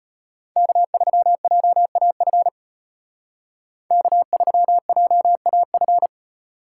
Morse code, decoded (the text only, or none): K3JAF K3JAF